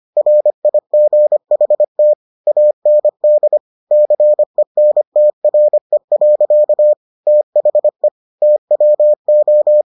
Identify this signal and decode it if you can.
Morse code, heard as a keyed tone, about 25 wpm, 600 Hz. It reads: RIGHT AND CENTRE. THE TWO